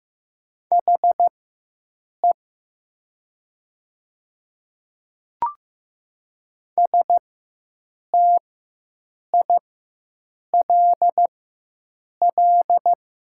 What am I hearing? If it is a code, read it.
Morse code, heard as a keyed tone, about 15 wpm, spaced as for 7 wpm, 700 Hz. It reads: HE STILL
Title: 4x Character Spacing - Top 300 Words - 15wpm